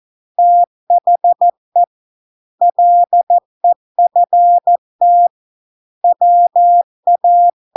Morse code, heard as a keyed tone, about 14 wpm, 700 Hz. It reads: THE LEFT WA